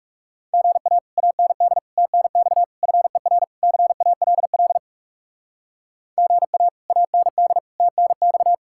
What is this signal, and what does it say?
Morse code, heard as a keyed tone, about 37 wpm, 700 Hz. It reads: GA AND TNX FER CALL GA AND TNX
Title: Speed-Racing Common QSO Phrases - 30wpm